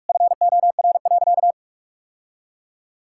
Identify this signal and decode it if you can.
Morse code, heard as a keyed tone, about 42 wpm, 700 Hz. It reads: FOR.
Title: Sentences from Top 1000 Words - Review Part 6 - 45wpm